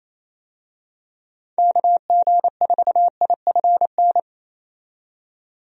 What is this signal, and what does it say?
Morse code, heard as a keyed tone, about 28 wpm, 700 Hz. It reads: KG4IFN
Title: USA Call Signs - 6 Characters - 28wpm